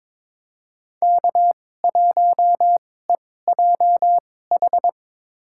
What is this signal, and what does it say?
Morse code, heard as a keyed tone, about 22 wpm, spaced as for 16 wpm, 700 Hz. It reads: K1EJH
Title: USA Call Signs - 2x Character Spacing - 22wpm